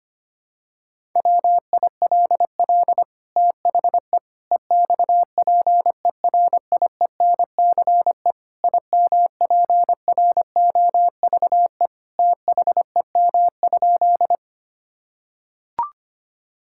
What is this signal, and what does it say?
Morse code, heard as a keyed tone, about 25 wpm, 700 Hz. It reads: WILL THE EXPERIENCE IMPROVE THEM?
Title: Sentences from Top 1000 Words - Review Part 5 - 25wpm